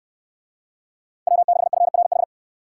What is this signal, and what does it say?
Morse code, heard as a keyed tone, about 62 wpm, 700 Hz. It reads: W6FRH